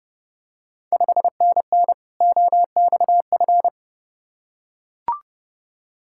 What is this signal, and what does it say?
Morse code, heard as a keyed tone, about 32 wpm, 700 Hz. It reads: 5NN OXF
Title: Ontario QSO Party Exchange 30wpm